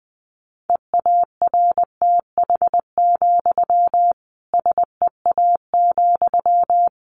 Morse code, heard as a keyed tone, about 20 wpm, 700 Hz. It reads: EARTH, SEA,
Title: Sentences from Top 1500 Words 20wpm